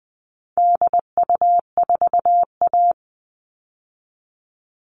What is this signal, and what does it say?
Morse code, heard as a keyed tone, about 19 wpm, 700 Hz. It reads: DU4A
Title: All Cut Numbers - 4-Digit 20wpm